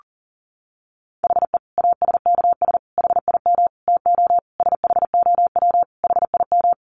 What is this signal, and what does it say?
Morse code, heard as a keyed tone, about 40 wpm, 700 Hz. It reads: HE ASKS HIM TO SHOW HIM
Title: Easy Phrases 40wpm